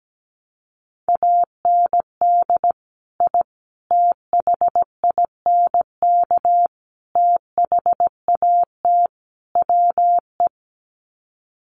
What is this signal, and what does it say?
Morse code, heard as a keyed tone, about 17 wpm, 700 Hz. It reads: AND I THINK THAT WE
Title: Sets of 5 Words Encore2 17wpm